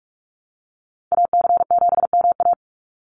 Morse code, heard as a keyed tone, about 44 wpm, 700 Hz. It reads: AC7MA
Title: NA SOTA Activator Callsigns 45wpm